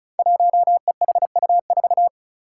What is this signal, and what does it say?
Morse code, heard as a keyed tone, about 36 wpm, 700 Hz. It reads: 1EHU4
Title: Five Challenging Letter-Number Combinations 35wpm